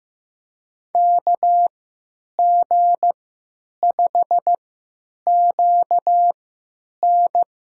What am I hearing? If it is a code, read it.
Morse code, heard as a keyed tone, about 16 wpm, spaced as for 9 wpm, 700 Hz. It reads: KG5QN
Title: USA Call Signs - 3x Character Spacing - 15wpm